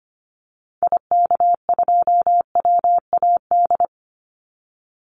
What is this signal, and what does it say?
Morse code, heard as a keyed tone, about 25 wpm, 700 Hz. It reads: IK2WAD
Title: Call Signs Repeated 3x - 25wpm